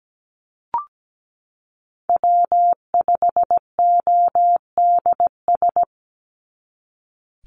Morse code, heard as a keyed tone, about 17 wpm, 700 Hz. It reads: W5ODS
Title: NA SOTA Activator Callsigns 17wpm